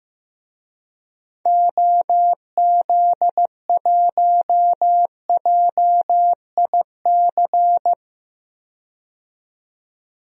Morse code, heard as a keyed tone, about 15 wpm, 700 Hz. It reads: OZ1JIC
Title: DXCC-100 Call Signs - 6 Characters - 15wpm